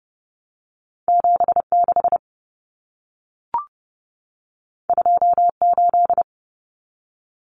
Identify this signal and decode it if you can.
Morse code, heard as a keyed tone, about 30 wpm, 700 Hz. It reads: 76 28
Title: Numbers - 2-Digit 30wpm